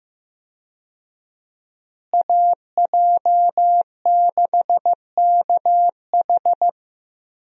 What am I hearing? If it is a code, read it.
Morse code, heard as a keyed tone, about 15 wpm, 700 Hz. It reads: AJ6KH
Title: USA Call Signs - 5 Characters - Repeated 3x - 15wpm